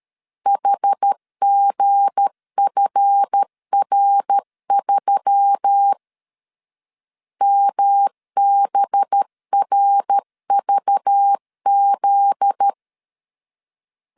Morse code, heard as a keyed tone, about 13 wpm, 800 Hz. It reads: HGFR3 MBRVZ
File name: LEZIONI / 7 GF9Z / 7 GF9Z 12wpm 2147